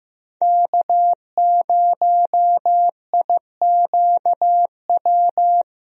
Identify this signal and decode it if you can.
Morse code, heard as a keyed tone, about 15 wpm, 700 Hz. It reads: K0IQW
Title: USA Call Signs - 5 Characters - Repeated 2x - 15wpm